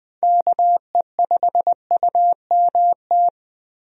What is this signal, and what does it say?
Morse code, heard as a keyed tone, about 20 wpm, 700 Hz. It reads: KE5UMT
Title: USA Call Signs - 6 Characters - Repeated 2x - 20wpm